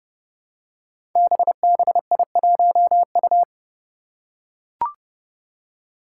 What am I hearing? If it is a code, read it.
Morse code, heard as a keyed tone, about 30 wpm, 700 Hz. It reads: BBI1U
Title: Five Challenging Letter-Number Combinations 30wpm